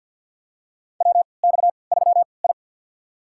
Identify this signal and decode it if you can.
Morse code, heard as a keyed tone, about 46 wpm, spaced as for 27 wpm, 700 Hz. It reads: WX3I
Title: USA Call Signs - 3x Character Spacing - 50wpm